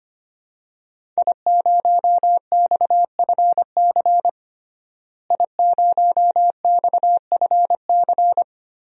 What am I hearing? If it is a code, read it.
Morse code, heard as a keyed tone, about 25 wpm, 700 Hz. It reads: I0XFC I0XFC
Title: DXCC-100 Call Signs - 5 Characters - Repeated 3x - 25wpm